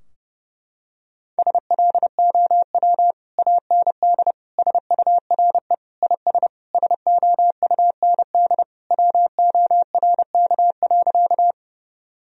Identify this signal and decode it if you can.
Morse code, heard as a keyed tone, about 30 wpm, 700 Hz. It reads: SLOW AND SURE IS SOUND WORK.
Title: Sentences from Top 1000 Words - Review Part 2 - 30wpm